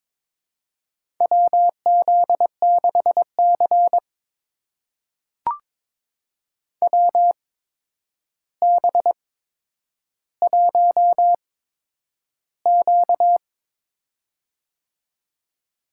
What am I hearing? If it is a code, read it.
Morse code, heard as a keyed tone, about 22 wpm, spaced as for 9 wpm, 700 Hz. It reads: WZ6C WB1Q
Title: USA Call Signs - 4 Characters - 8x Character Spacing - 22wpm